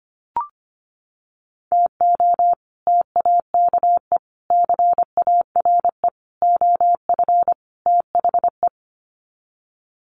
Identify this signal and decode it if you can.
Morse code, heard as a keyed tone, about 25 wpm, 700 Hz. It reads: TO TAKE CARE OF THE
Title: Sets of 5 Words Encore 25wpm